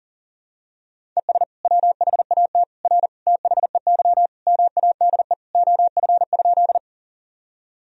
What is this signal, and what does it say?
Morse code, heard as a keyed tone, about 40 wpm, 700 Hz. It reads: ES WHAT R THEY MADE OF?
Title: Sentences from Top 200 Words and Abbreviations 40wpm